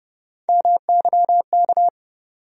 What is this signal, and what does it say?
Morse code, heard as a keyed tone, about 30 wpm, 700 Hz. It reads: MYK
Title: USA Call Sign Postfixes - 3 Letters 30wpm